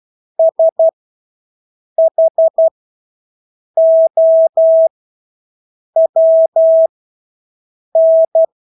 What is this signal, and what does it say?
Morse code, heard as a keyed tone, about 12 wpm, spaced as for 6 wpm, 650 Hz. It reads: SHOWN
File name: Engelske ord 12-6 wpm